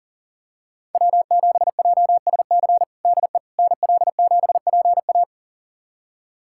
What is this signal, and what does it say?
Morse code, heard as a keyed tone, about 37 wpm, 700 Hz. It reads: W7JSC DE NL7PA
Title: USA Callsign DE Callsign - 5 Characters - 40wpm